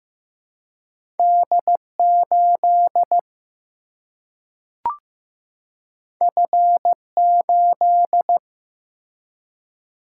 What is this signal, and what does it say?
Morse code, heard as a keyed tone, about 15 wpm, 700 Hz. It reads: D8 F8
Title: Single-Letter Single-Number 15wpm